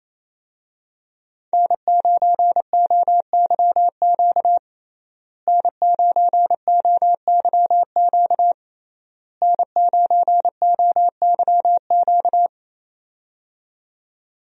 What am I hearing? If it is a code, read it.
Morse code, heard as a keyed tone, about 28 wpm, 700 Hz. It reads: N9OYQ N9OYQ N9OYQ
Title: USA Call Signs - 5 Characters - Repeated 3x - 28wpm